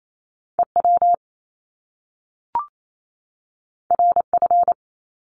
Morse code, heard as a keyed tone, about 28 wpm, 700 Hz. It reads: EW RF
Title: Top Two Letter Combinations in Words 28wpm